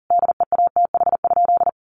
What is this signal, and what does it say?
Morse code, heard as a keyed tone, about 40 wpm, 700 Hz. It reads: DEATH?
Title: Sentences from Top 1100 Words 40wpm